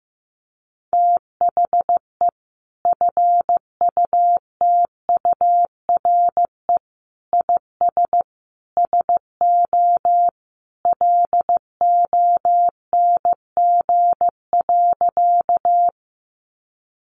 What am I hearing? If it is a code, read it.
Morse code, heard as a keyed tone, about 15 wpm, 700 Hz. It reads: THE FUTURE IS SO LONG.
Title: Sentences from Top 700 Words 15wpm